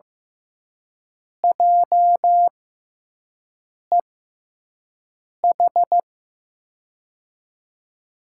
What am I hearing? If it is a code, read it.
Morse code, heard as a keyed tone, about 15 wpm, spaced as for 5 wpm, 700 Hz. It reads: JEH
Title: USA Call Signs - 6 Characters - 6x Character Spacing - 15wpm